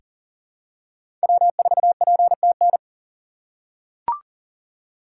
Morse code, heard as a keyed tone, about 40 wpm, 700 Hz. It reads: W4PTN
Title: USA Call Signs - 5 Characters - 4x Character Spacing - 40wpm